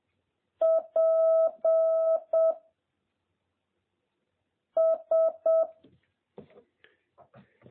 Morse code, heard as a keyed tone, about 7 wpm, 650 Hz. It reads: P S